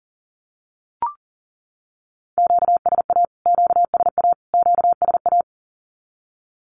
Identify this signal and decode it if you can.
Morse code, heard as a keyed tone, about 40 wpm, 700 Hz. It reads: QSA QSA QSA
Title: Q-Codes Repeated 3x 40wpm